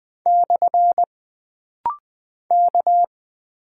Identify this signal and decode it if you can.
Morse code, heard as a keyed tone, about 20 wpm, 700 Hz. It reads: / K
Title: Single Character (Letter:Number:Punctuation:Prosign) - Rapid-Fire - 20wpm